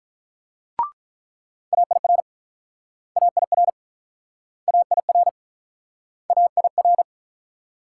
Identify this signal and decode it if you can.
Morse code, heard as a keyed tone, about 43 wpm, 700 Hz. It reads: AIR AIR AIR AIR
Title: Speed-Racing Multi-letter Phonemes - 35wpm